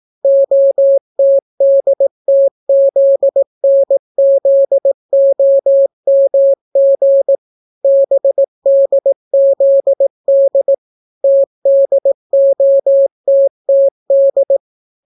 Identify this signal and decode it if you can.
Morse code, heard as a keyed tone, about 18 wpm, 550 Hz. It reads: OTDTZNZOMG BDZD TDOTTD